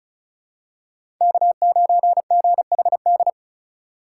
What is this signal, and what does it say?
Morse code, heard as a keyed tone, about 34 wpm, 700 Hz. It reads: K9GHD
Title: USA Call Signs - 5 Characters - 4x Character Spacing - 35wpm